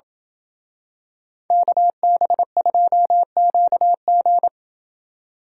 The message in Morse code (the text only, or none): KB2QG